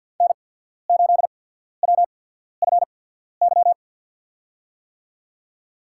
Morse code, heard as a keyed tone, about 48 wpm, spaced as for 14 wpm, 700 Hz. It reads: N8WFY
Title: USA Call Signs - 5 Characters - 8x Character Spacing - 50wpm